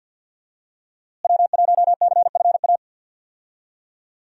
Morse code, heard as a keyed tone, about 52 wpm, 700 Hz. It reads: W1CFA
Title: USA Call Signs - 5 Characters - Repeated 3x - 50wpm